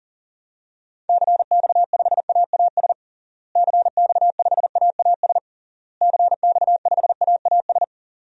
Morse code, heard as a keyed tone, about 40 wpm, 700 Hz. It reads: CX5AAS CX5AAS CX5AAS
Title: DXCC-100 Call Signs - 6 Characters - Repeated 3x - 40wpm